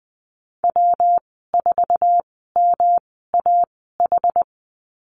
Morse code, heard as a keyed tone, about 20 wpm, spaced as for 15 wpm, 700 Hz. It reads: W4MAH
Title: 140 Lesson 23g - Letter f - Call Signs 20wpm